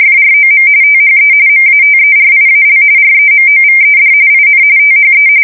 RTTY, about 45.45 baud. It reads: FROM THE MATERIAL REMOVED IN THE
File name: fsk170 / USS-Massachusetts